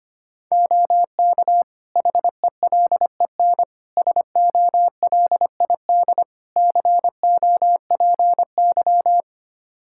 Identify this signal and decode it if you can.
Morse code, heard as a keyed tone, about 25 wpm, 700 Hz. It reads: OK HELEN SOLID COPY
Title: Practice QSO - Exchange 3 - More Info - 25wpm